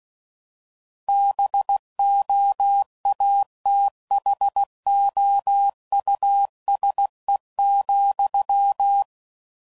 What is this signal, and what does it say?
Morse code, heard as a keyed tone, about 16 wpm, 800 Hz. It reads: BOATHOUSE,